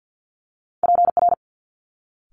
Morse code, heard as a keyed tone, about 50 wpm, 700 Hz. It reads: PR
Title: Multi-letter Phonemes Quick Fire 50wpm